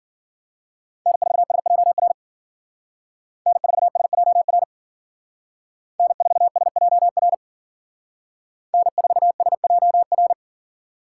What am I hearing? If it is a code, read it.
Morse code, heard as a keyed tone, about 48 wpm, 700 Hz. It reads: N4SJR N4SJR N4SJR N4SJR
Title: Speed Racing - USA Call Signs 40wpm